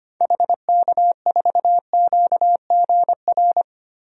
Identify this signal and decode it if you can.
Morse code, heard as a keyed tone, about 25 wpm, 700 Hz. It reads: HK4QGR
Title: DXCC-100 Call Signs - 6 Characters - 25wpm